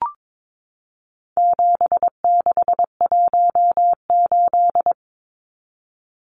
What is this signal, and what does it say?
Morse code, heard as a keyed tone, about 22 wpm, 700 Hz. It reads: 7618
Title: Numbers - 4-Digit 22wpm